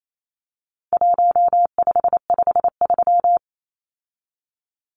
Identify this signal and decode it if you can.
Morse code, heard as a keyed tone, about 28 wpm, 700 Hz. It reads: 1553
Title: Numbers - 4-Digit 28wpm